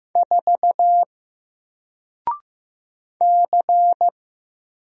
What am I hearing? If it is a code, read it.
Morse code, heard as a keyed tone, about 15 wpm, 700 Hz. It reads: 4 C
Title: Single Character (Letter:Number:Punctuation:Prosign) Alt 15wpm